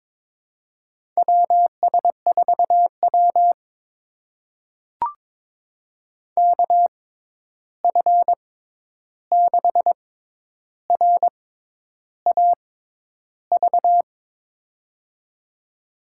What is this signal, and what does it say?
Morse code, heard as a keyed tone, about 21 wpm, spaced as for 10 wpm, 700 Hz. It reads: WS4W KF6RAV
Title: USA Call Signs - 6x Character Spacing - 22wpm